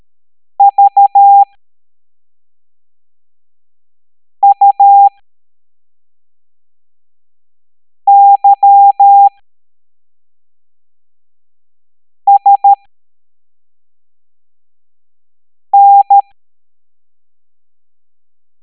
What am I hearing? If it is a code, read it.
Morse code, heard as a keyed tone, about 13 wpm, spaced as for 3 wpm, 800 Hz. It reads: VUYSN
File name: hamdata / L05-EX01-SVY